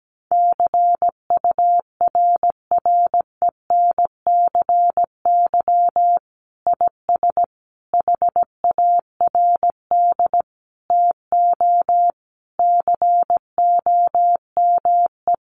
Morse code, heard as a keyed tone, about 17 wpm, 700 Hz. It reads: CURRENCY IS HARD TO COME
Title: Sentences from Top 1500 Words 17wpm